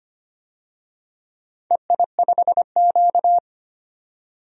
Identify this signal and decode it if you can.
Morse code, heard as a keyed tone, about 25 wpm, 700 Hz. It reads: EI5Q
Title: DXCC-100 Call Signs - 4 Characters - 25wpm